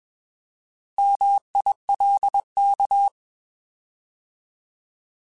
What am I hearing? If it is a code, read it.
Morse code, heard as a keyed tone, about 21 wpm, 800 Hz. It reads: MILK
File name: Alphabetical / Alphabetical mp3s / Ms @20